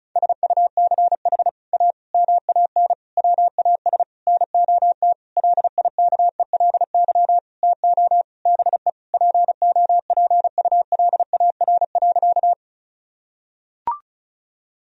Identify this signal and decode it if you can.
Morse code, heard as a keyed tone, about 36 wpm, 700 Hz. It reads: SUCH A MAN WAS NOT LIKELY TO BE POPULAR.
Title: Sentences from Top 1000 Words - Review Part 4 - 35wpm